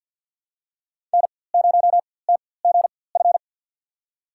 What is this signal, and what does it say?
Morse code, heard as a keyed tone, about 47 wpm, spaced as for 23 wpm, 700 Hz. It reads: N0TGF
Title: USA Call Signs - 4x Character Spacing - 50wpm